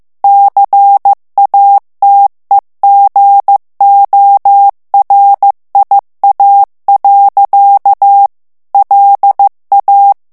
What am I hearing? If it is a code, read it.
Morse code, heard as a keyed tone, about 15 wpm, 800 Hz. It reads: CATEGORIA. LA
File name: morse mp3 / ITALIEN texte clair 2 vitesse 14 mots minute